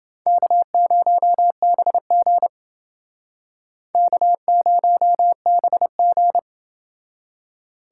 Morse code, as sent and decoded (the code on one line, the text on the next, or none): -.- ----- -... --. / -.- ----- -... --.
K0BG K0BG